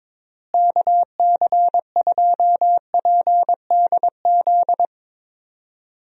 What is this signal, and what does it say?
Morse code, heard as a keyed tone, about 22 wpm, 700 Hz. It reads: KC2PDZ
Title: USA Call Signs - 6 Characters - Repeated 3x - 22wpm